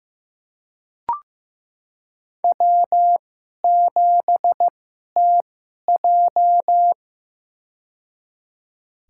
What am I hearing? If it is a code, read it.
Morse code, heard as a keyed tone, about 15 wpm, spaced as for 11 wpm, 700 Hz. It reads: W7TJ
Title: USA Call Signs - 2x Character Spacing - 15wpm